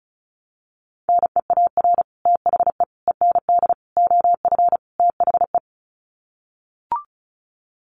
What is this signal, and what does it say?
Morse code, heard as a keyed tone, about 34 wpm, 700 Hz. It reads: NEAR THE END OF THE
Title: Sets of 5 Words Encore 35wpm